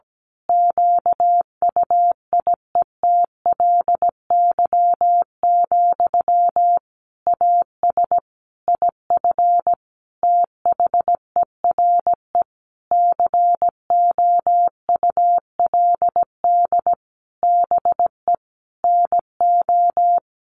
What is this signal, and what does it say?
Morse code, heard as a keyed tone, about 17 wpm, 700 Hz. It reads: QUIETLY, AS IF THERE COULD BE NO
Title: A Sound Of Thunder 17wpm